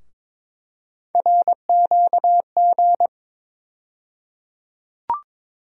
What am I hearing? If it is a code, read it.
Morse code, heard as a keyed tone, about 22 wpm, 700 Hz. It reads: RQG
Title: USA Call Sign Postfixes - 3 Letters 22wpm